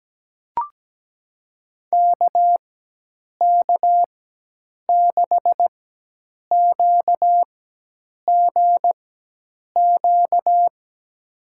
Morse code, heard as a keyed tone, about 17 wpm, spaced as for 8 wpm, 700 Hz. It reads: KK6QGQ
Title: USA Call Signs - 6 Characters - 4x Character Spacing - 17wpm